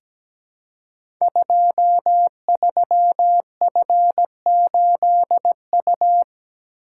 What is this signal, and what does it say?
Morse code, heard as a keyed tone, about 17 wpm, 700 Hz. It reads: 23F8U
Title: Code Groups - 5 Letter:Numbers - 17wpm